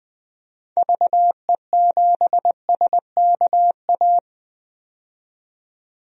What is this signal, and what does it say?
Morse code, heard as a keyed tone, about 20 wpm, 700 Hz. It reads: VE7SKA
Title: DXCC-100 Call Signs - 6 Characters - 20wpm